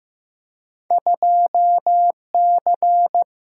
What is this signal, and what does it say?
Morse code, heard as a keyed tone, about 15 wpm, 700 Hz. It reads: 2 C
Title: Single-Letter Single-Number 15wpm